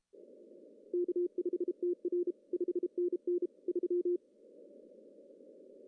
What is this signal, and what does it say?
Morse code, heard as a keyed tone, about 32 wpm, 350 Hz. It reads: K5TR 5NN 3
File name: CW / cqww-cw-k5tr